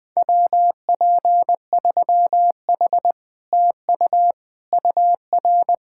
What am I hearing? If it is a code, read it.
Morse code, heard as a keyed tone, about 20 wpm, 700 Hz. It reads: WP3H TU UR
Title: Speed-Racing Common QSO Phrases - 17wpm